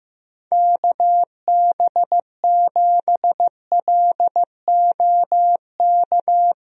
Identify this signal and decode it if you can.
Morse code, heard as a keyed tone, about 15 wpm, 700 Hz. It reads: KB7LOK
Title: USA Call Signs - 6 Characters - 15wpm